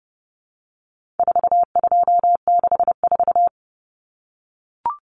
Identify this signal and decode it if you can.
Morse code, heard as a keyed tone, about 30 wpm, 700 Hz. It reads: 4264
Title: Numbers - 4-Digit 30wpm